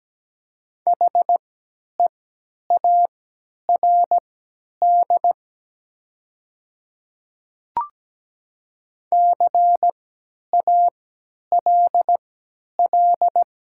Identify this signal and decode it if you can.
Morse code, heard as a keyed tone, about 17 wpm, spaced as for 10 wpm, 690 Hz. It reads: HEARD CALL
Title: 3x Character Spacing - Top 500 Words - 17wpm